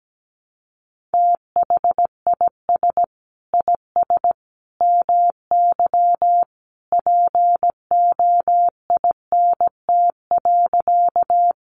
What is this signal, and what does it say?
Morse code, heard as a keyed tone, about 17 wpm, 700 Hz. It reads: THIS IS MY POINT.
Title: Sentences from Top 200 Words 17wpm